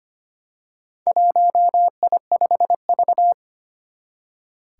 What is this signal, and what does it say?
Morse code, heard as a keyed tone, about 25 wpm, 700 Hz. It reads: 1I5V
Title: Four Challenging Letter-Number Combinations 25wpm